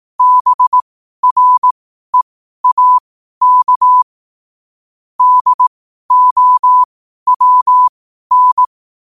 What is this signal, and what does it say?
Morse code, heard as a keyed tone, about 18 wpm, spaced as for 12 wpm, 1000 Hz. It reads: BREAK DOWN